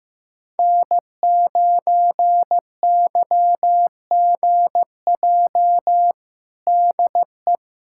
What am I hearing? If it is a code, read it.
Morse code, heard as a keyed tone, about 15 wpm, 700 Hz. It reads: N9YGJ DE
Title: USA Callsign DE Callsign 15wpm